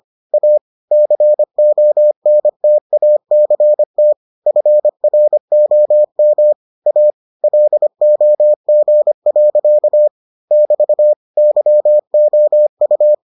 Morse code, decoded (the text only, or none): A CONTACT FROM A LOG. <BT> YOU